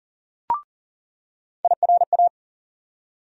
Morse code, heard as a keyed tone, about 40 wpm, 700 Hz. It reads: IRA
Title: USA Call Sign Postfixes - 3 Letters 40wpm